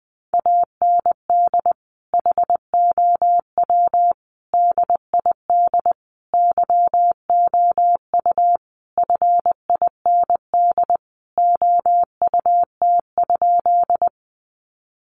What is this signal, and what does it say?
Morse code, heard as a keyed tone, about 20 wpm, 700 Hz. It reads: AND HOW DID YOU FIND OUT?
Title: Sentences from Top 100 Words 20wpm